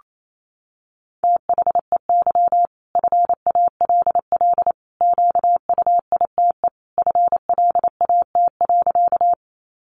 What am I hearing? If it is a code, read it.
Morse code, heard as a keyed tone, about 28 wpm, 700 Hz. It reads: THEY FALL QUITE FLAT.
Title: Sentences from Top 1000 Words 28wpm